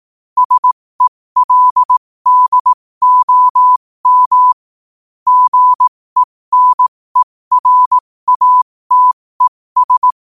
Morse code, heard as a keyed tone, about 18 wpm, spaced as for 15 wpm, 1000 Hz. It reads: SELDOM GENERATES
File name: qotdfiles / 15wpm